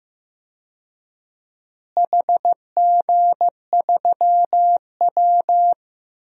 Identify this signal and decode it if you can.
Morse code, heard as a keyed tone, about 15 wpm, 700 Hz. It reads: HG3W
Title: DXCC-100 Call Signs - 4 Characters - Repeated 3x - 15wpm